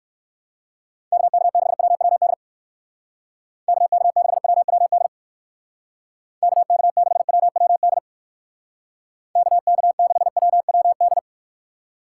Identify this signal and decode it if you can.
Morse code, heard as a keyed tone, about 64 wpm, 700 Hz. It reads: KK6WWD KK6WWD KK6WWD KK6WWD